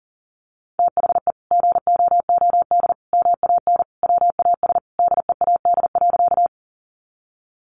Morse code, heard as a keyed tone, about 40 wpm, 700 Hz. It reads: THE GOOD MAN WAS DEAD.
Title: Sentences from Top 1500 Words 40wpm